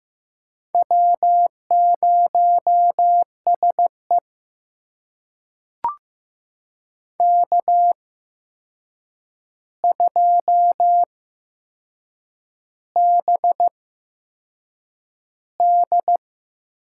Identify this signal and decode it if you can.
Morse code, heard as a keyed tone, about 15 wpm, spaced as for 6 wpm, 700 Hz. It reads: W0SE K2BD
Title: USA Call Signs - 4 Characters - 8x Character Spacing - 15wpm